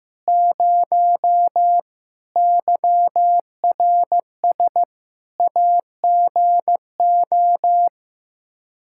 Morse code, heard as a keyed tone, about 15 wpm, 700 Hz. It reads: 0 YRS AGO
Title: K1USN SST QSO Practice 15wpm